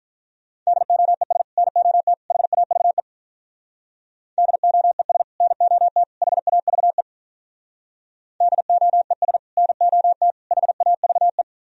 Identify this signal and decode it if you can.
Morse code, heard as a keyed tone, about 44 wpm, 700 Hz. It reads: DOES NOT HAVE DOES NOT HAVE DOES NOT HAVE